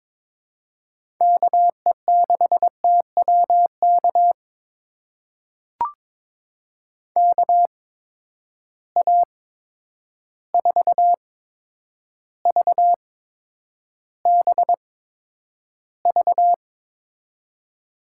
Morse code, heard as a keyed tone, about 22 wpm, spaced as for 9 wpm, 700 Hz. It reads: KE6TWK KA4VBV
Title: USA Call Signs - 8x Character Spacing - 22wpm